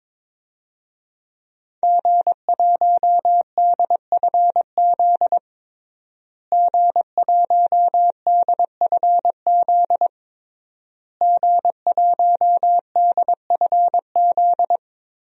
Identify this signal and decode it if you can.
Morse code, heard as a keyed tone, about 21 wpm, 700 Hz. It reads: G1DFZ G1DFZ G1DFZ
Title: DXCC-100 Call Signs - 5 Characters - Repeated 3x - 22wpm